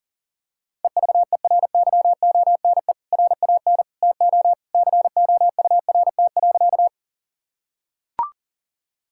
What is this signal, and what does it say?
Morse code, heard as a keyed tone, about 40 wpm, 700 Hz. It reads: EVERYONE RAN TO COURT.
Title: Sentences from Top 1600 Words 40wpm